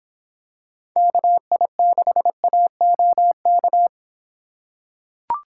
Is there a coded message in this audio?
Yes: Morse code from 1.0 to 3.9 s, heard as a keyed tone, about 25 wpm, 700 Hz. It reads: KI6AOK